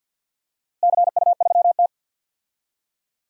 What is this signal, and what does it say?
Morse code, heard as a keyed tone, about 52 wpm, 700 Hz. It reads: CU3T
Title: DXCC-100 Call Signs - 4 Characters - Repeated 2x - 50wpm